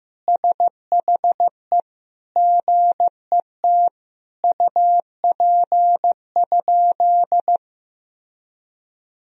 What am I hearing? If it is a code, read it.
Morse code, heard as a keyed tone, about 15 wpm, 700 Hz. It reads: SHE GET UP?
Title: Sentences from Top 1000 Words - Review Part 1 - 15wpm